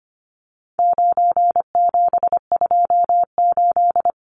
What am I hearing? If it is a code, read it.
Morse code, heard as a keyed tone, about 25 wpm, 700 Hz. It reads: 9728
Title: Numbers - 4-Digit 25wpm